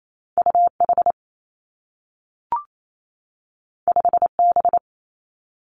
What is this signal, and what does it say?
Morse code, heard as a keyed tone, about 28 wpm, 700 Hz. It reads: U H 5 B
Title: Two Challenging Letter-Number Combinations 28wpm